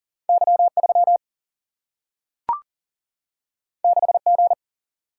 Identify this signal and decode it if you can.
Morse code, heard as a keyed tone, about 41 wpm, 700 Hz. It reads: Y3 6G
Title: Single-Letter Single-Number 40wpm